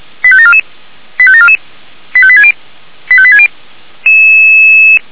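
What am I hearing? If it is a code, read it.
Five-tone selective call (ZVEI), heard as quick runs of five tones, tones of about 70 ms. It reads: 85744 85744 85860 85860
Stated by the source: ZVEI selective call received by SDR, from feuerwehr-orlamuende.org fire brigade audio archive.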